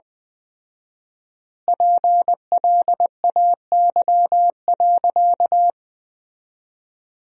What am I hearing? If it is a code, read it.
Morse code, heard as a keyed tone, about 20 wpm, 700 Hz. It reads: PLAY.
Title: 4x Word-Spacing - Sentences from Top 200 Words - 20wpm